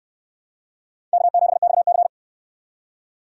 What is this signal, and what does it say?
Morse code, heard as a keyed tone, about 68 wpm, 700 Hz. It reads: K7XC